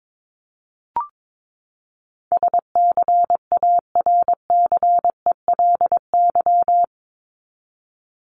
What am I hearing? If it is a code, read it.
Morse code, heard as a keyed tone, about 22 wpm, 700 Hz. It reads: SCARCELY
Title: LY Suffix - Top 100 Words with Base Words 22wpm